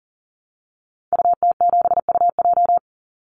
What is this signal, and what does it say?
Morse code, heard as a keyed tone, about 40 wpm, 700 Hz. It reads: UT7UJ
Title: Call Signs Repeated 2x - 40wpm